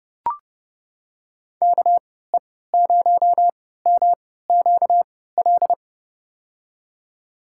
Morse code, heard as a keyed tone, about 30 wpm, spaced as for 17 wpm, 700 Hz. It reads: KE0MQL
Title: USA Call Signs - 6 Characters - 3x Character Spacing - 30wpm